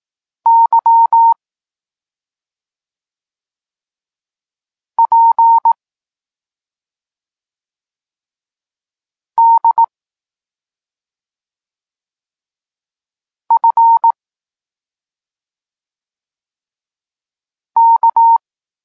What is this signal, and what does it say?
Morse code, heard as a keyed tone, about 18 wpm, 920 Hz. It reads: Y P D F K
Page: Tekst 32 – Doskonalenie odbioru poznanych liter